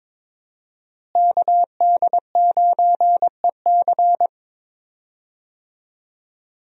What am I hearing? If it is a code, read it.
Morse code, heard as a keyed tone, about 22 wpm, 700 Hz. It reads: KD9EC
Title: USA Call Signs - 5 Characters - Repeated 2x - 22wpm